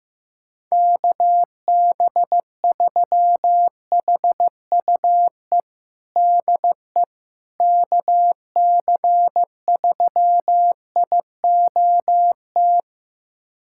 Morse code, heard as a keyed tone, about 15 wpm, 700 Hz. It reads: KB3HUE DE KC3IOT
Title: USA Callsign DE Callsign - 6 Characters - 15wpm